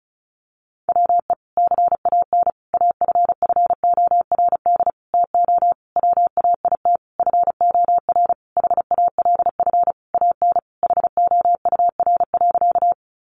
Morse code, heard as a keyed tone, about 34 wpm, 700 Hz. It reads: WE CAN AFFORD TO WAIT FOR HALF AN HOUR.
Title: Sentences from Top 1100 Words 35wpm